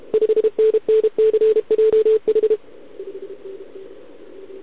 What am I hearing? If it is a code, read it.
Morse code, heard as a keyed tone, about 32 wpm, 440 Hz. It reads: 5NNCJH
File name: ARCHIV / OKDX2020 / OK1RIb